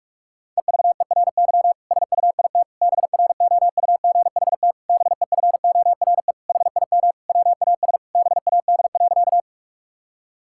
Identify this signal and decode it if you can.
Morse code, heard as a keyed tone, about 46 wpm, 700 Hz. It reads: EVERY SUIT BROUGHT BEFORE HIM WAS BAD.
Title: Sentences from Top 1600 Words 45wpm